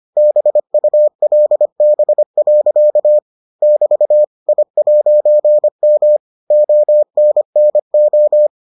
Morse code, heard as a keyed tone, about 25 wpm, 600 Hz. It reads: BULB. <BT> I'M ONNO